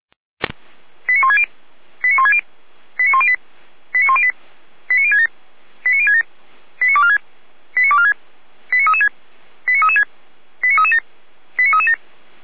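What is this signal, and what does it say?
Five-tone selective call (ZVEI), heard as quick runs of five tones, tones of about 70 ms. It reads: 89160 89160 89108 89108 89067 89067 89246 89246 89336 89336 89337 89337
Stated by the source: ZVEI five-tone selective call received on SDR.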